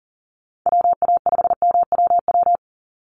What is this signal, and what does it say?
Morse code, heard as a keyed tone, about 40 wpm, 700 Hz. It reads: WA5MWW
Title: USA Call Signs 40wpm